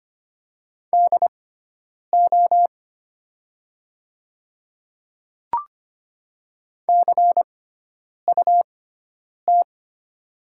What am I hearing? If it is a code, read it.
Morse code, heard as a keyed tone, about 25 wpm, spaced as for 9 wpm, 700 Hz. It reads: DO CUT
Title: 6x Character Spacing - Top 300 Words - 25wpm